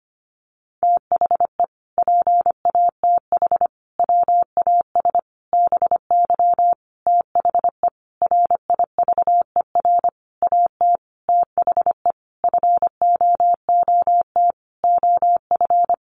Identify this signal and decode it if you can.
Morse code, heard as a keyed tone, about 25 wpm, 700 Hz. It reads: THE PATH WAS BY THE RIVER AT THE FOOT OF
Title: Sentences from Top 1400 Words 25wpm